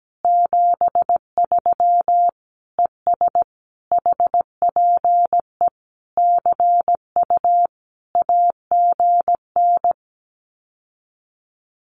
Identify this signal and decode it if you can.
Morse code, heard as a keyed tone, about 17 wpm, 700 Hz. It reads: 73 ES HPE CU AGN
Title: Common QSO Phrases Encore 17wpm